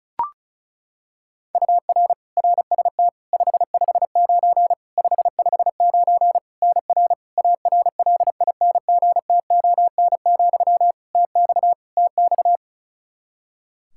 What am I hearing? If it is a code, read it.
Morse code, heard as a keyed tone, about 36 wpm, 700 Hz. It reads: UR RST 559 559 NR ARLINGTON, TX TX
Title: Practice QSO - Exchange 1 - Beginning - 35wpm